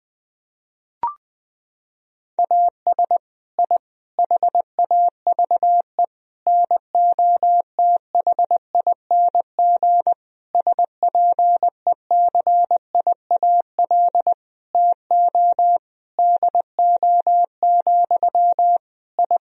Morse code, heard as a keyed tone, about 19 wpm, 700 Hz. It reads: AS I HAVE NOTHING SPECIAL TO DO, I
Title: Sentences from Top 1000 Words - Review Part 3 - 20wpm